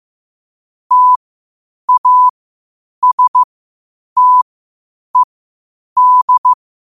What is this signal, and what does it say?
Morse code, heard as a keyed tone, about 15 wpm, spaced as for 8 wpm, 1000 Hz. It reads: TASTED
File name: qotdfiles / 8wpm